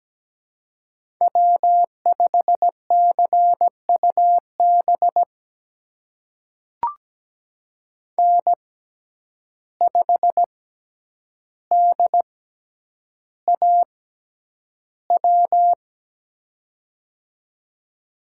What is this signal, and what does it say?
Morse code, heard as a keyed tone, about 17 wpm, spaced as for 9 wpm, 700 Hz. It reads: W5CUB N5DAW
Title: USA Call Signs - 5 Characters - 6x Character Spacing - 17wpm